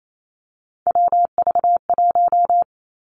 Morse code, heard as a keyed tone, about 28 wpm, 700 Hz. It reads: WV1
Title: Three Challenging Letter-Number Combinations 28wpm